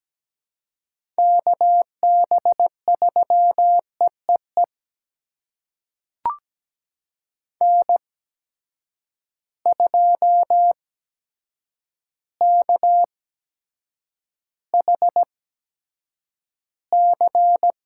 Morse code, heard as a keyed tone, about 17 wpm, spaced as for 8 wpm, 700 Hz. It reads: KB3EEE N2KHC
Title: USA Call Signs - 8x Character Spacing - 17wpm